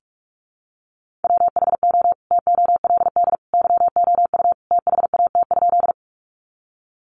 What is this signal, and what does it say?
Morse code, heard as a keyed tone, about 46 wpm, 700 Hz. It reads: WHO TOLD YOU THAT?
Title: Sentences from Top 400 Words 45wpm